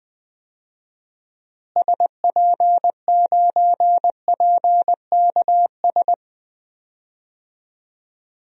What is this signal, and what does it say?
Morse code, heard as a keyed tone, about 20 wpm, 700 Hz. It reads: SP9PKS
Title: DXCC-100 Call Signs - 6 Characters - Repeated 2x - 20wpm